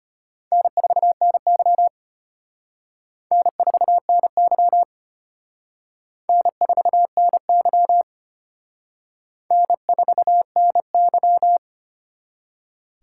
Morse code, heard as a keyed tone, about 32 wpm, 700 Hz. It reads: N4NY N4NY N4NY N4NY